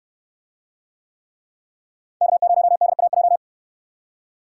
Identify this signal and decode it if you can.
Morse code, heard as a keyed tone, about 69 wpm, 700 Hz. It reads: K0DAJ